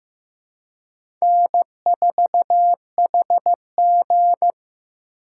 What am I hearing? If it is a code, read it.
Morse code, heard as a keyed tone, about 15 wpm, 700 Hz. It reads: N4HG
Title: USA Call Signs - 4 Characters - 2x Character Spacing - 15wpm